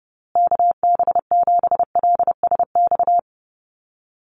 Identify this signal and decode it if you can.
Morse code, heard as a keyed tone, about 30 wpm, 700 Hz. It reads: KB7LSX
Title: USA Call Signs Repeated 2x 30wpm